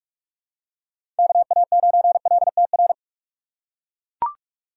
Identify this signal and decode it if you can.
Morse code, heard as a keyed tone, about 42 wpm, 700 Hz. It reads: KA9LTR
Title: USA Call Signs - 6 Characters - 2x Character Spacing - 45wpm